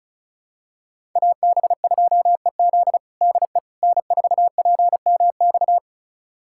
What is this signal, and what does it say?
Morse code, heard as a keyed tone, about 36 wpm, 690 Hz. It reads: AB2EZ DE N4PMX
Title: USA Callsign DE Callsign - 5 Characters - 35wpm